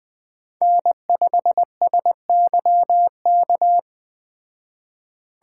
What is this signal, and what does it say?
Morse code, heard as a keyed tone, about 21 wpm, 700 Hz. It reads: N5SYK